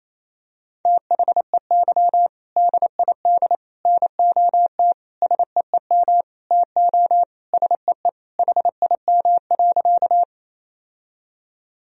Morse code, heard as a keyed tone, about 28 wpm, 700 Hz. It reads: THEY DID NOT SEEM TO SEE HIM.
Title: Sentences from Top 1000 Words - Review Part 1 - 28wpm